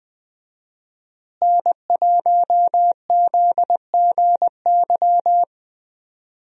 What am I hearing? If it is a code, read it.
Morse code, heard as a keyed tone, about 20 wpm, 700 Hz. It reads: N1ZGY
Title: USA Call Signs - 5 Characters - Repeated 2x - 20wpm